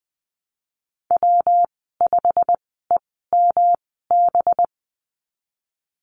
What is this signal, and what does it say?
Morse code, heard as a keyed tone, about 20 wpm, spaced as for 15 wpm, 700 Hz. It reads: W5EMB
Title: 219 Lesson 35g - Number 0 - Call Signs 20wpm